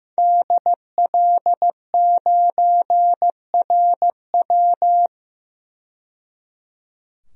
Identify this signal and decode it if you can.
Morse code, heard as a keyed tone, about 15 wpm, 700 Hz. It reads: DL9RW
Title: DXCC-100 Call Signs - 5 Characters - 15wpm